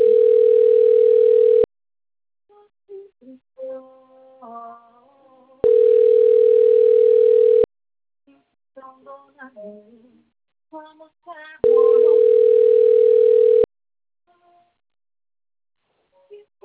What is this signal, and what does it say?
Ringback tone (North America), 440 + 480 Hz.